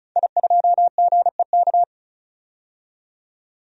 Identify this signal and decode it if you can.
Morse code, heard as a keyed tone, about 36 wpm, 700 Hz. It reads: I2GEK
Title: DXCC-100 Call Signs - 5 Characters - 35wpm